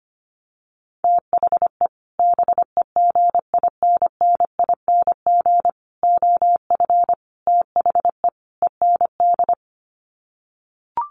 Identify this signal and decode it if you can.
Morse code, heard as a keyed tone, about 25 wpm, 700 Hz. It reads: THE BEGINNING OF THE END
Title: Sets of 5 Words Encore2 25wpm